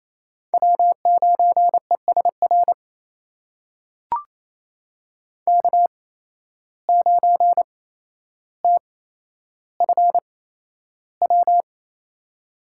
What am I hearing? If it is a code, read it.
Morse code, heard as a keyed tone, about 28 wpm, spaced as for 12 wpm, 700 Hz. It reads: W9ESR K9TFW
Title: USA Call Signs - 5 Characters - 8x Character Spacing - 28wpm